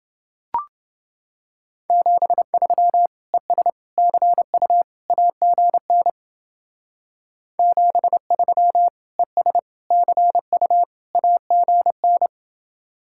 Morse code, heard as a keyed tone, about 29 wpm, 700 Hz. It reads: 73 ES CU AGN 73 ES CU AGN